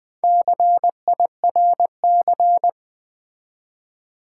Morse code, heard as a keyed tone, about 20 wpm, 700 Hz. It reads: CIRC
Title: Build A Word - 4 Letters At A Time - 11-Letter Words - 20wpm